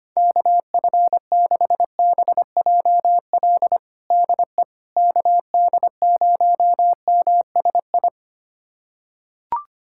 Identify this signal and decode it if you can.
Morse code, heard as a keyed tone, about 25 wpm, 700 Hz. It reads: KF6BJL DE KD0MSI
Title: USA Callsign DE Callsign - 6 Characters - 25wpm